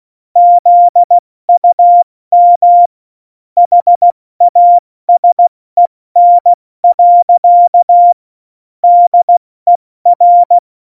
Morse code, heard as a keyed tone, about 16 wpm, 700 Hz. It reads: ZUM HASEN. DER
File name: Morsekurs / Witze / Witz 08. Dezember 2020, 70 F80